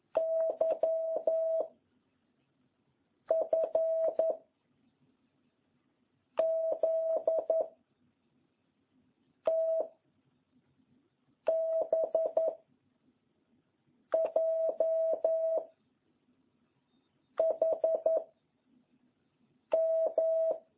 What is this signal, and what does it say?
Morse code, heard as a keyed tone, about 10 wpm, 650 Hz. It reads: Y F Z T B J H M